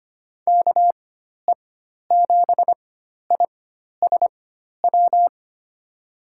Morse code, heard as a keyed tone, about 25 wpm, spaced as for 12 wpm, 700 Hz. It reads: KE7ISW
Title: USA Call Signs - 6 Characters - 4x Character Spacing - 25wpm